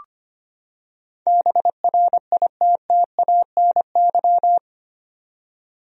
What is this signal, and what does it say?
Morse code, heard as a keyed tone, about 25 wpm, 700 Hz. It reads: BRITTANY
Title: USA Common Names - 25wpm